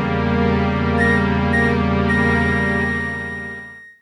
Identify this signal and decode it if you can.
There is no code.